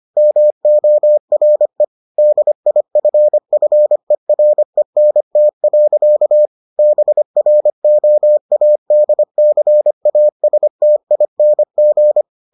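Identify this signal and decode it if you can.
Morse code, heard as a keyed tone, about 25 wpm, 600 Hz. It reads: MORE DIFFERENT. BROADCASTING
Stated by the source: Morse edition of the Foundations of Amateur Radio episodes.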